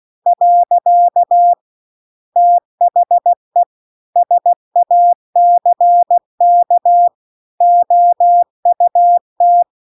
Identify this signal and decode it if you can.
Morse code, heard as a keyed tone, about 16 wpm, 700 Hz. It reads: . THE SACK OUT
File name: CW-The-Big-Twoheaded-River-2-16WPM